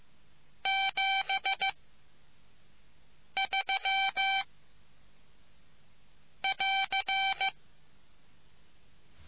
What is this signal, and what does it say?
Morse code, heard as a keyed tone, about 15 wpm, spaced as for 5 wpm, 2350 Hz. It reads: 73 <AR>